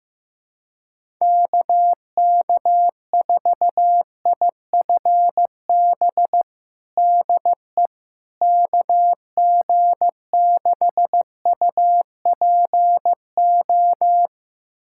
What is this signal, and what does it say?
Morse code, heard as a keyed tone, about 15 wpm, 700 Hz. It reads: KK4IFB DE KG6UPO
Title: USA Callsign DE Callsign - 6 Characters - 15wpm